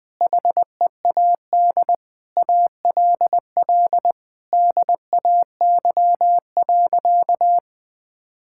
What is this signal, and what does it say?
Morse code, heard as a keyed tone, about 21 wpm, 700 Hz. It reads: HEAD ALL DAY.
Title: Difficult Sentences with Hints - 20wpm